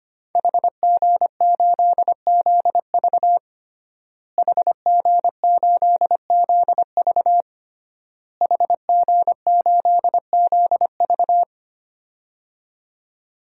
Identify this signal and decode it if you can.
Morse code, heard as a keyed tone, about 25 wpm, 700 Hz. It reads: HG8ZV HG8ZV HG8ZV
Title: DXCC-100 Call Signs - 5 Characters - Repeated 3x - 25wpm